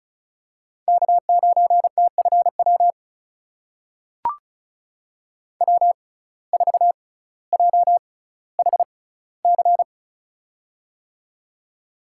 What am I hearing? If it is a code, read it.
Morse code, heard as a keyed tone, about 36 wpm, spaced as for 18 wpm, 700 Hz. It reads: K9TFW W4JHC
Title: USA Call Signs - 6x Character Spacing - 35wpm